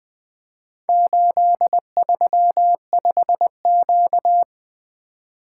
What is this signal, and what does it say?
Morse code, heard as a keyed tone, about 20 wpm, 700 Hz. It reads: 835Q
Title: Four Letter-Number Combinations - 20wpm